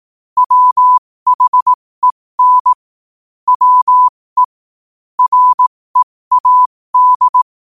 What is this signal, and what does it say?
Morse code, heard as a keyed tone, about 18 wpm, spaced as for 15 wpm, 1000 Hz. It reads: WHEN WE READ
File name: qotdfiles / 15wpm